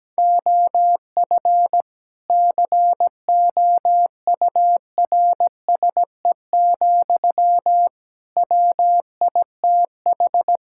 Morse code, heard as a keyed tone, about 17 wpm, 700 Hz. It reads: OF COURSE, WITH
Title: Sentences from Top 1600 Words 17wpm